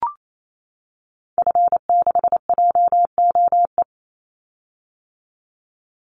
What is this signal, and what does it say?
Morse code, heard as a keyed tone, about 28 wpm, 700 Hz. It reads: F6JOE
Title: Call Signs - 28wpm